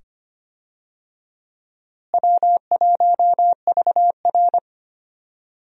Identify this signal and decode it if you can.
Morse code, heard as a keyed tone, about 25 wpm, 700 Hz. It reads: W1VR
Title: USA Call Signs - 4 Characters - Repeated 2x - 25wpm